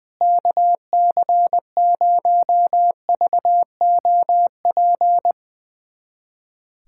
Morse code, heard as a keyed tone, about 20 wpm, 700 Hz. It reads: KC0VOP
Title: USA Callsign DE Callsign 20wpm